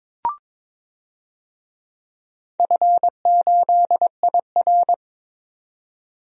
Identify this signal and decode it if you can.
Morse code, heard as a keyed tone, about 22 wpm, 700 Hz. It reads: F8IR
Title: DXCC-100 Call Signs - 4 Characters - 22wpm